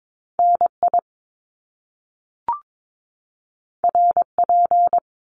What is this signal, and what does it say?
Morse code, heard as a keyed tone, about 22 wpm, 700 Hz. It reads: NI RP
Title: Top Two Letter Combinations in Words 22wpm